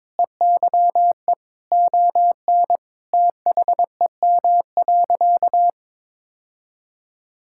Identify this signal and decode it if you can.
Morse code, heard as a keyed tone, about 22 wpm, 700 Hz. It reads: EYE ON THEM.
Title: Sentences from Top 1600 Words 22wpm